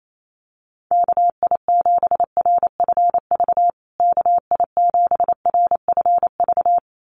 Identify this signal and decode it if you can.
Morse code, heard as a keyed tone, about 28 wpm, 700 Hz. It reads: KI7RFV KI7RFV
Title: USA Call Signs Repeated 2x 28wpm